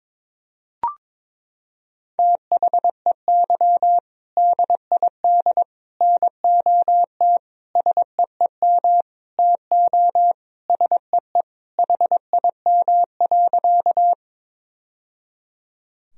Morse code, heard as a keyed tone, about 22 wpm, 700 Hz. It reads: THEY DID NOT SEEM TO SEE HIM.
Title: Sentences from Top 1000 Words - Review Part 1 - 22wpm